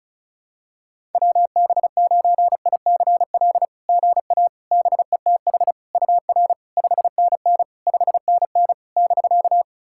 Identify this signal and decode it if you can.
Morse code, heard as a keyed tone, about 34 wpm, 690 Hz. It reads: WB9ICL GA BETH UR 5NN 5NN <BK>
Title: SOTA QSO Practice 35wpm